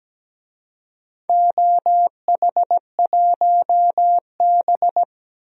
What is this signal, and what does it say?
Morse code, heard as a keyed tone, about 17 wpm, 700 Hz. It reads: OH1B
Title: DXCC-100 Call Signs - 4 Characters - 17wpm